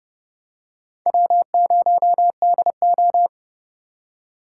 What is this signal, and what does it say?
Morse code, heard as a keyed tone, about 30 wpm, 700 Hz. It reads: W0DO
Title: USA Call Signs - 4 Characters - 3x Character Spacing - 30wpm